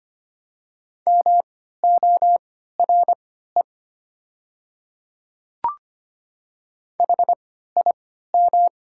Morse code, heard as a keyed tone, about 25 wpm, spaced as for 14 wpm, 700 Hz. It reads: MORE HIM
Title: 3x Character Spacing - Top 100 Words - 25wpm